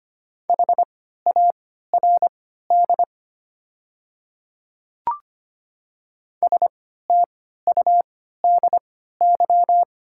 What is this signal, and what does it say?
Morse code, heard as a keyed tone, about 25 wpm, spaced as for 14 wpm, 700 Hz. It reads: HARD STUDY
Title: 3x Character Spacing - Top 300 Words - 25wpm